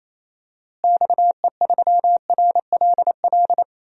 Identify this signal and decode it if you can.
Morse code, heard as a keyed tone, about 28 wpm, 700 Hz. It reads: XE3RLL
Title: DXCC-100 Call Signs - 6 Characters - Repeated 3x - 28wpm